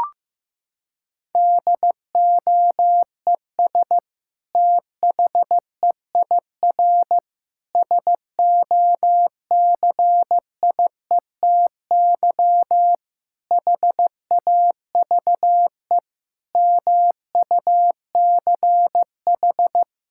Morse code, heard as a keyed tone, about 15 wpm, 700 Hz. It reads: DOES THEIR SOCIETY HAVE MUCH
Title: Sentences from Top 1000 Words - Review Part 6 - 15wpm